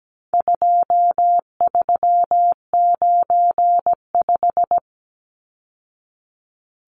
Morse code, heard as a keyed tone, about 17 wpm, 700 Hz. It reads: 2395
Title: Numbers - 4-Digit 17wpm